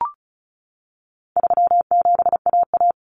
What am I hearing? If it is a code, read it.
Morse code, heard as a keyed tone, about 34 wpm, 700 Hz. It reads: 37AA
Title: Common Cut Numbers - 4-Digit 35wpm